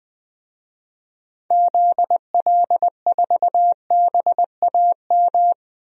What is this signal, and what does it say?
Morse code, heard as a keyed tone, about 20 wpm, 700 Hz. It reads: ZL4BAM
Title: DXCC-100 Call Signs - 6 Characters - 20wpm